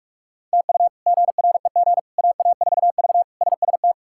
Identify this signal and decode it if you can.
Morse code, heard as a keyed tone, about 46 wpm, 700 Hz. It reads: TU GREG AA4V SST
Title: K1USN SST QSO Practice - 45wpm